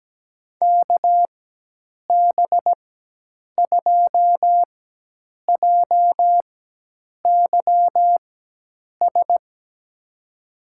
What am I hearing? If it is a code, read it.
Morse code, heard as a keyed tone, about 17 wpm, spaced as for 8 wpm, 700 Hz. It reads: KB2JYS
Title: USA Call Signs - 4x Character Spacing - 17wpm